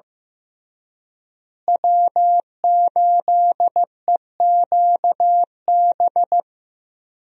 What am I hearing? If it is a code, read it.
Morse code, heard as a keyed tone, about 15 wpm, 700 Hz. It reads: W8EQB
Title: USA Call Signs - 5 Characters - Repeated 2x - 15wpm